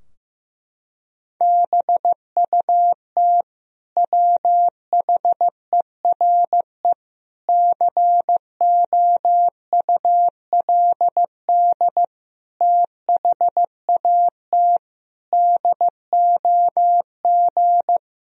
Morse code, heard as a keyed tone, about 16 wpm, 700 Hz. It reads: BUT WHERE COULD THAT DOG
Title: Sentences from Top 1000 Words - Review Part 2 - 15wpm